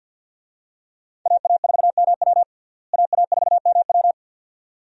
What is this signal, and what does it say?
Morse code, heard as a keyed tone, about 51 wpm, 700 Hz. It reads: AA4MW AA4MW
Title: USA Call Signs - 5 Characters - Repeated 2x - 50wpm